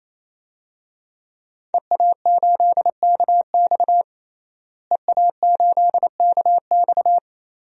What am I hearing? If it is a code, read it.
Morse code, heard as a keyed tone, about 27 wpm, 700 Hz. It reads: EA8KX EA8KX
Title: DXCC-100 Call Signs - 5 Characters - Repeated 3x - 28wpm